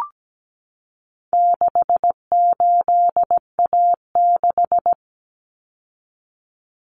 Morse code, heard as a keyed tone, about 17 wpm, 700 Hz. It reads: 68A6
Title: Common Cut Numbers - 4-Digit 17wpm